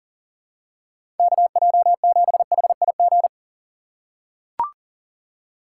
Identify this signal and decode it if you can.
Morse code, heard as a keyed tone, about 40 wpm, 700 Hz. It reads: KJ7HIG
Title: USA Call Signs - 6 Characters - 6x Character Spacing - 40wpm